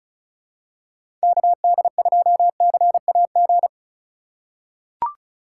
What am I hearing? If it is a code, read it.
Morse code, heard as a keyed tone, about 36 wpm, 700 Hz. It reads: KD2CAG
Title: USA Call Signs - 6 Characters - 4x Character Spacing - 35wpm